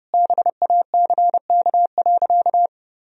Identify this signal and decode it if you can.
Morse code, heard as a keyed tone, about 32 wpm, 700 Hz. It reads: BACK.
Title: Sentences from Top 1000 Words - Review Part 1 - 30wpm